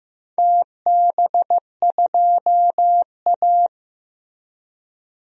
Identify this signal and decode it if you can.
Morse code, heard as a keyed tone, about 15 wpm, 700 Hz. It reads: TB2A
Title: DXCC-100 Call Signs - 4 Characters - Repeated 3x - 15wpm